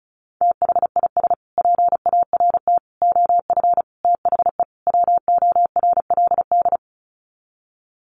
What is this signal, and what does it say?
Morse code, heard as a keyed tone, about 34 wpm, 700 Hz. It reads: THIS PART OF THE WORLD
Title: Sets of 5 Words Encore2 35wpm